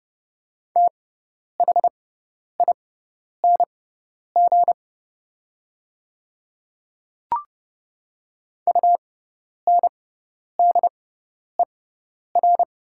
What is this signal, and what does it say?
Morse code, heard as a keyed tone, about 30 wpm, spaced as for 10 wpm, 700 Hz. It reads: THING UNDER
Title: 6x Character Spacing - Top 300 Words - 30wpm